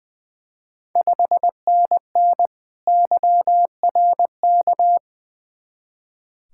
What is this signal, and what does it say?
Morse code, heard as a keyed tone, about 20 wpm, 700 Hz. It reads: 5NN YRK
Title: Ontario QSO Party Exchange 20wpm